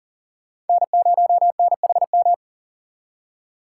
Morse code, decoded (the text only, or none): N0NHM